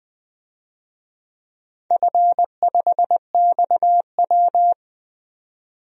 Morse code, heard as a keyed tone, about 20 wpm, 700 Hz. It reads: F5XW
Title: DXCC-100 Call Signs - 4 Characters - 20wpm